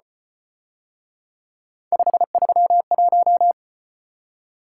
Morse code, heard as a keyed tone, about 32 wpm, 700 Hz. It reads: 531